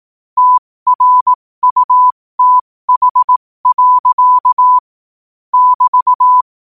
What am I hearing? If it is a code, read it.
Morse code, heard as a keyed tone, about 18 wpm, spaced as for 15 wpm, 1000 Hz. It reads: TRUTH. <BT>
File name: qotdfiles / 15wpm